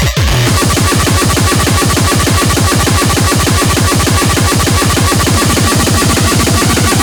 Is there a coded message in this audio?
No.